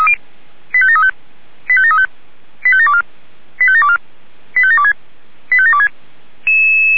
Five-tone selective call (ZVEI), heard as quick runs of five tones, tones of about 80 ms. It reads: 85735 85735 85724 85724 85726 85726